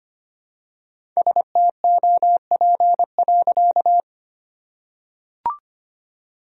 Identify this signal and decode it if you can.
Morse code, heard as a keyed tone, about 25 wpm, 700 Hz. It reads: STOP.
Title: Sentences from Top 1000 Words - Review Part 1 - 25wpm